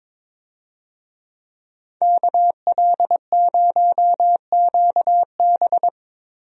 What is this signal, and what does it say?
Morse code, heard as a keyed tone, about 22 wpm, 700 Hz. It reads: KL0QB
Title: USA Call Signs - 5 Characters - 22wpm